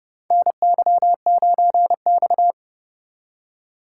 Morse code, heard as a keyed tone, about 30 wpm, 700 Hz. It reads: NY9X
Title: K1USN SST QSO Practice - 30wpm